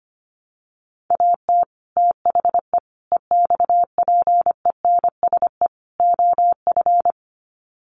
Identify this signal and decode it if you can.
Morse code, heard as a keyed tone, about 25 wpm, 700 Hz. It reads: AT THE EXPENSE OF
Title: Sets of 4 Words 25wpm